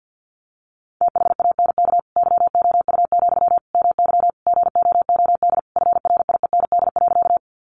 Morse code, heard as a keyed tone, about 50 wpm, 700 Hz. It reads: THANK YOU, MY GOOD FRIEND.
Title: Sentences from Top 600 Words 50wpm